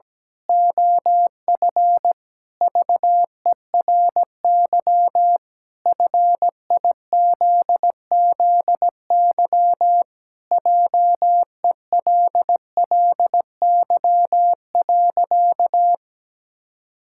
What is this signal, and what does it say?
Morse code, heard as a keyed tone, about 17 wpm, 700 Hz. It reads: OF VERY FIZZY JELLY.
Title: Pangram Challenge v2 - 17wpm